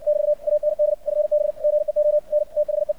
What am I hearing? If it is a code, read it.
Morse code, heard as a keyed tone, about 57 wpm, 600 Hz. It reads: <BT> AND FB GEO N THE
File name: archived / CRSnet